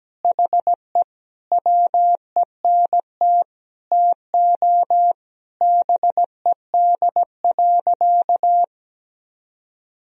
Morse code, heard as a keyed tone, about 17 wpm, 700 Hz. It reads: HE WENT TO BED.
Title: Sentences from Top 1000 Words - Review Part 3 - 17wpm